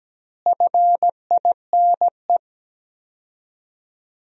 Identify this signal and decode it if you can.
Morse code, heard as a keyed tone, about 17 wpm, 700 Hz. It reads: FINE
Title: 4x Word-Spacing - Sentences from Top 500 Words - 17wpm